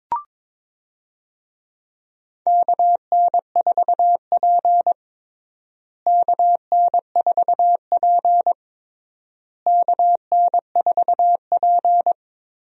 Morse code, heard as a keyed tone, about 22 wpm, 700 Hz. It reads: KN4P KN4P KN4P
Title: USA Call Signs - 4 Characters - Repeated 3x - 22wpm